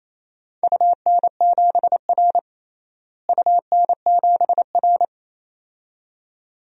Morse code, heard as a keyed tone, about 27 wpm, 700 Hz. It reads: UN7R UN7R
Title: DXCC-100 Call Signs - 4 Characters - Repeated 2x - 28wpm